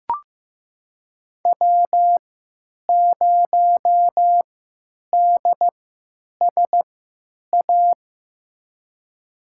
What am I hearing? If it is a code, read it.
Morse code, heard as a keyed tone, about 15 wpm, spaced as for 8 wpm, 700 Hz. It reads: W0DSA
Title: USA Call Signs - 3x Character Spacing - 15wpm